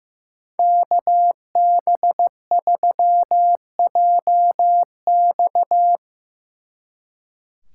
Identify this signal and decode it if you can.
Morse code, heard as a keyed tone, about 17 wpm, 700 Hz. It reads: KB3JX
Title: Speed Racing - USA 2x2 Call Signs 15wpm